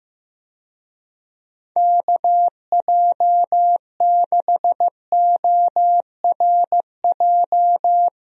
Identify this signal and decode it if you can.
Morse code, heard as a keyed tone, about 16 wpm, 700 Hz. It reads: KJ6ORJ
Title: USA Call Signs - 6 Characters - Repeated 2x - 15wpm